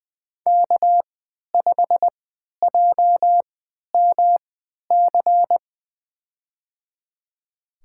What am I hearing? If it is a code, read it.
Morse code, heard as a keyed tone, about 20 wpm, spaced as for 11 wpm, 700 Hz. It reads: K5JMC
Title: USA Call Signs - 5 Characters - 3x Character Spacing - 20wpm